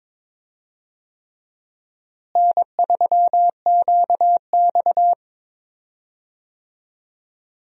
Morse code, heard as a keyed tone, about 22 wpm, 700 Hz. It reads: N3QX
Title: USA Call Signs - 4 Characters - Repeated 3x - 22wpm